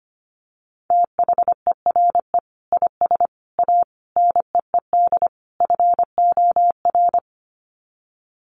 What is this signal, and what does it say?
Morse code, heard as a keyed tone, about 25 wpm, 700 Hz. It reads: THERE IS A NEED FOR
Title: Sets of 5 Words Encore2 25wpm